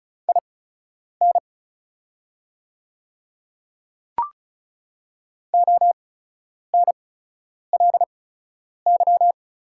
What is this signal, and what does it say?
Morse code, heard as a keyed tone, about 36 wpm, spaced as for 10 wpm, 700 Hz. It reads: IN ONLY
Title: 8x Character Spacing - Top 200 Words - 35wpm